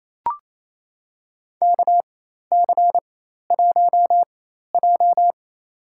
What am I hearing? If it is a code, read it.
Morse code, heard as a keyed tone, about 27 wpm, spaced as for 13 wpm, 700 Hz. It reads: KC1J
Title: USA Call Signs - 4 Characters - 4x Character Spacing - 28wpm